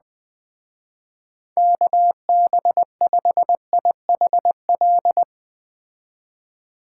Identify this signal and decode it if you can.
Morse code, heard as a keyed tone, about 20 wpm, 700 Hz. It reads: KB5IHL